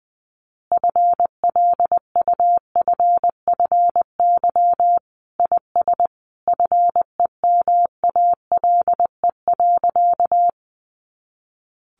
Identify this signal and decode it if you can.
Morse code, heard as a keyed tone, about 20 wpm, 700 Hz. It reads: FLUFFY IS FEMALE.
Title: Easy Phrases 20wpm